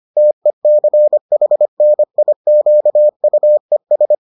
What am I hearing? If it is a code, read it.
Morse code, heard as a keyed tone, about 25 wpm, 600 Hz. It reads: TECHNIQUES